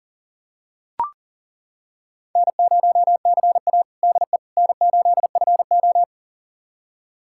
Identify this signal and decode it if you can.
Morse code, heard as a keyed tone, about 40 wpm, 690 Hz. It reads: N0CA DE N8FO
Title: USA Callsign DE Callsign - 4 Characters - 40wpm